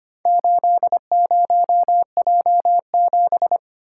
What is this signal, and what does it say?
Morse code, heard as a keyed tone, about 25 wpm, 700 Hz. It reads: 80J7
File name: practice / 25wpm / Four Letter-Number Combinations - 25wpm